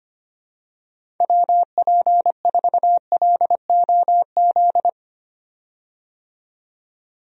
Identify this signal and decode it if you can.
Morse code, heard as a keyed tone, about 25 wpm, 700 Hz. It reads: WP4LOZ
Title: USA Call Signs - 6 Characters - 25wpm